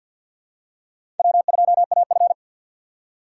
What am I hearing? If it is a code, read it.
Morse code, heard as a keyed tone, about 52 wpm, 700 Hz. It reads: W2AF
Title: USA Call Signs - 4 Characters - 6x Character Spacing - 50wpm